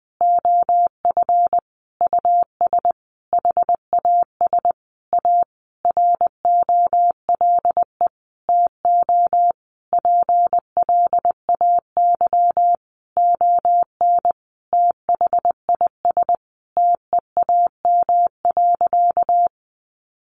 Morse code, heard as a keyed tone, about 20 wpm, 700 Hz. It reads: OF US HAS A ROLE TO PLAY ON THIS TEAM.
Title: Sentences from Top 1500 Words 20wpm